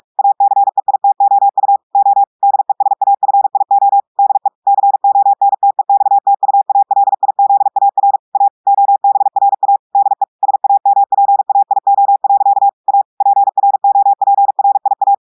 Morse code, heard as a keyed tone, about 46 wpm, 800 Hz. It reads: ACEITOU O DESAFIO DE CONTEXTUALIZAR A OBRA DE SAMPAIO. A PROPRIA
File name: audios / portugais / portugais texte en clair 1 vitesse 45 mots minutes